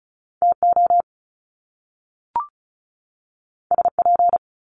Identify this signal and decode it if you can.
Morse code, heard as a keyed tone, about 36 wpm, 700 Hz. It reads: TO SP
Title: Top Two Letter Combinations in Words 35wpm